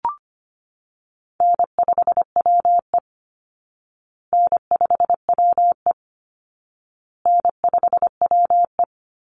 Morse code, heard as a keyed tone, about 25 wpm, 700 Hz. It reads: N5WE N5WE N5WE
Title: Easy Call Signs Repeated 3x - 25wpm